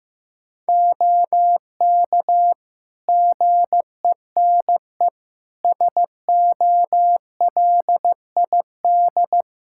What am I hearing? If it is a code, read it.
Morse code, heard as a keyed tone, about 15 wpm, 700 Hz. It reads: OK GENE SOLID
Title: Practice QSO - Exchange 3 - More Info - 15wpm